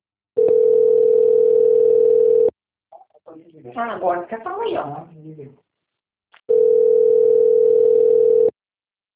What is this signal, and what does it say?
Ringback tone (North America), 440 + 480 Hz, with speech.
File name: out- - - - - .20447